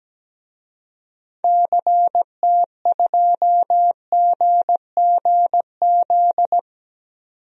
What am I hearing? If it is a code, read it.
Morse code, heard as a keyed tone, about 17 wpm, 700 Hz. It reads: CT2GGZ
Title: DXCC-100 Call Signs - 6 Characters - 17wpm